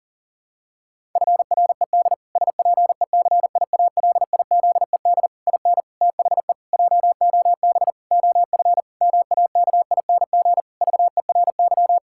Morse code, heard as a keyed tone, about 37 wpm, 700 Hz. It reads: FRED SPECIALIZED IN THE JOB OF MAKING VERY
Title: Pangram Challenge v2 - 40wpm